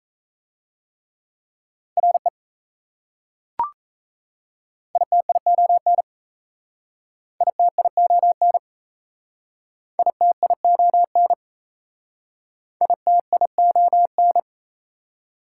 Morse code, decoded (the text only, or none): RE ITION ITION ITION ITION